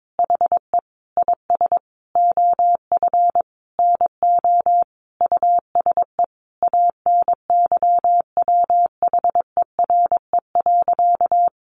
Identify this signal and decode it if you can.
Morse code, heard as a keyed tone, about 22 wpm, 700 Hz. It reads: HE IS OF NO USE ANYWHERE.
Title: Sentences from Top 1400 Words 22wpm